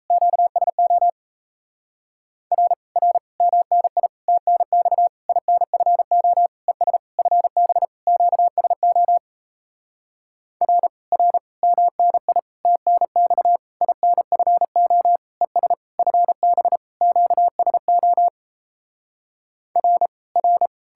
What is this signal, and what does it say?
Morse code, heard as a keyed tone, about 37 wpm, 700 Hz. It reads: QSO R R MNI TNX INFO ES FB QSO R R MNI TNX INFO ES FB QSO R R